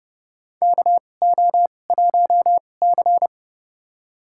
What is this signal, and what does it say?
Morse code, heard as a keyed tone, about 30 wpm, spaced as for 22 wpm, 700 Hz. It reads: KO1C
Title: USA Call Signs - 4 Characters - 2x Character Spacing - 30wpm